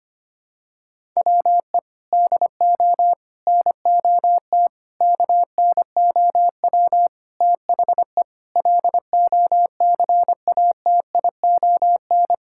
Morse code, heard as a keyed tone, about 25 wpm, 700 Hz. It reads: WE DO NOT KNOW THE LOCATION
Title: Sentences from Top 1000 Words - Review Part 6 - 25wpm